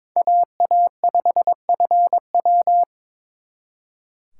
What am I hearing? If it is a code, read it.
Morse code, heard as a keyed tone, about 26 wpm, 700 Hz. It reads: AA5FW
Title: Speed Racing - USA 2x2 Call Signs 22wpm